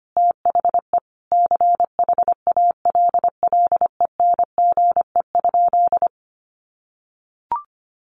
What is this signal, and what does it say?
Morse code, heard as a keyed tone, about 25 wpm, 700 Hz. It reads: THE CHALLENGE?
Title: Sentences from Top 1300 Words 25wpm